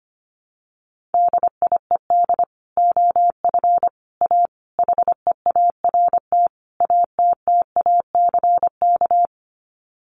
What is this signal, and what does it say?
Morse code, heard as a keyed tone, about 25 wpm, 700 Hz. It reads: DIED OF A HEART ATTACK
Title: Sets of 5 Words Encore2 25wpm